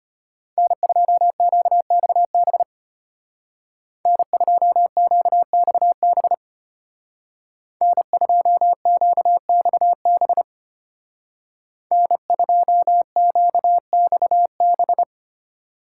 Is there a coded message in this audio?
Yes: Morse code from 0.6 to 15.0 s, heard as a keyed tone, about 33 wpm, 700 Hz. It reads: N2QXB N2QXB N2QXB N2QXB